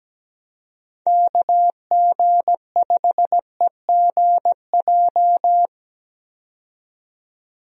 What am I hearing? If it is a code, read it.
Morse code, heard as a keyed tone, about 17 wpm, 700 Hz. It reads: KG5EGJ
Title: USA Call Signs - 6 Characters - 17wpm